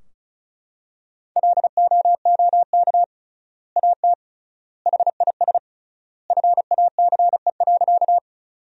Morse code, heard as a keyed tone, about 36 wpm, 700 Hz. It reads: LOOK AT HIS FACE.
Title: 3x Word-Spacing - Sentences from Top 300 Words - 35wpm